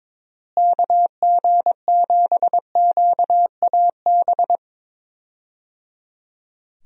Morse code, heard as a keyed tone, about 22 wpm, 700 Hz. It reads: KG7QAB
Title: USA Call Signs - 6 Characters - Repeated 3x - 22wpm